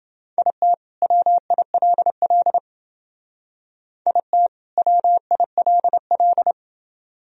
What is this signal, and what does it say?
Morse code, heard as a keyed tone, about 29 wpm, 700 Hz. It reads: IT WILL IT WILL